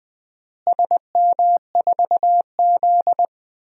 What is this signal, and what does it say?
Morse code, heard as a keyed tone, about 20 wpm, 700 Hz. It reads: SM4Z
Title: DXCC-100 Call Signs - 4 Characters - Repeated 2x - 20wpm